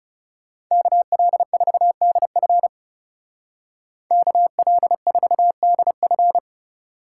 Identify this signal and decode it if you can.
Morse code, heard as a keyed tone, about 39 wpm, 700 Hz. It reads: KL4DF KL4DF